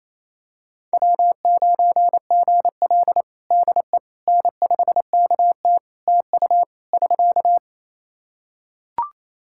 Morse code, heard as a keyed tone, about 28 wpm, 700 Hz. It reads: W9GL DE N5KT TU <SK>
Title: Practice QSO - Exchange 4 - Ending - 28wpm